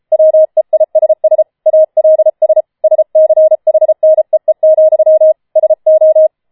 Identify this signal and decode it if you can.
Morse code, heard as a keyed tone, about 33 wpm, 630 Hz. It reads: WEISS ALS SCHNEE, SO
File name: Schneewittchen Teil5 34wpm